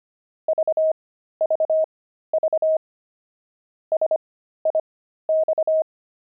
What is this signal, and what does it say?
Morse code, heard as a keyed tone, about 25 wpm, spaced as for 13 wpm, 630 Hz. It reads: VVV SIX
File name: pangram-fs-13-cs-25-javelins